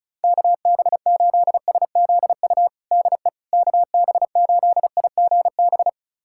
Morse code, heard as a keyed tone, about 34 wpm, 700 Hz. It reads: KB8SZU DE KB8IGB
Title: USA Callsign DE Callsign - 6 Characters - 35wpm